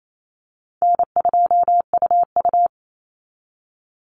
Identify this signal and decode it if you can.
Morse code, heard as a keyed tone, about 28 wpm, 700 Hz. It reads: N2UU
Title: Easy Call Signs - 28wpm